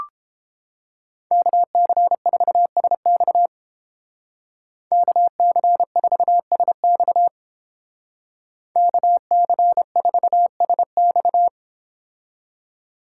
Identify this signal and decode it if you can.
Morse code, heard as a keyed tone, about 29 wpm, 700 Hz. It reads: KC4SX KC4SX KC4SX